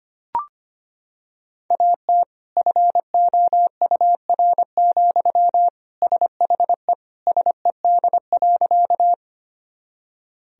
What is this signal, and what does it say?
Morse code, heard as a keyed tone, about 25 wpm, 700 Hz. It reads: AT FOUR, SHE SED.
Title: Sentences from Top 300 Words and Abbreviations 25wpm